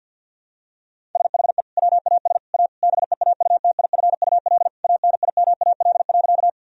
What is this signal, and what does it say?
Morse code, heard as a keyed tone, about 50 wpm, 700 Hz. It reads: SHE WAS A BEAUTIFUL ANIMAL.
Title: Sentences from Top 1000 Words - Review Part 1 - 50wpm